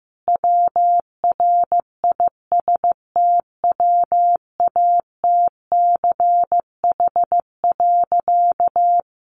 Morse code, heard as a keyed tone, about 15 wpm, 700 Hz. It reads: WRISTWATCH.
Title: A Sound Of Thunder 15wpm